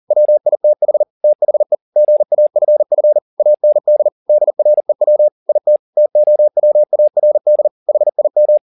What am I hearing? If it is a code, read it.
Morse code, heard as a keyed tone, about 40 wpm, 600 Hz. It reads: WITH THE GAFF AND DREW IT TOWARD HIM